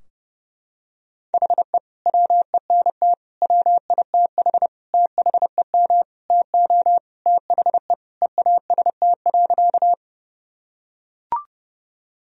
Morse code, heard as a keyed tone, about 30 wpm, 700 Hz. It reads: HE WENT WITH THEM TO THE EAST.
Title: Sentences from Top 1000 Words - Review Part 3 - 30wpm